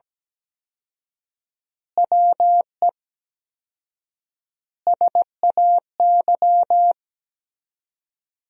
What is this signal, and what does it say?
Morse code, heard as a keyed tone, about 17 wpm, 700 Hz. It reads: WE SAY
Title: 4x Word-Spacing - Sentences from Top 200 Words - 17wpm